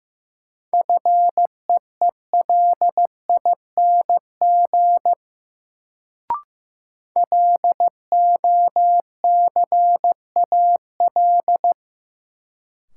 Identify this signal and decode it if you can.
Morse code, heard as a keyed tone, about 15 wpm, 700 Hz. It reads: FEELING LOCAL
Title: Sentences from Top 1000 Words - Review Part 3 - New Word Drill - 15wpm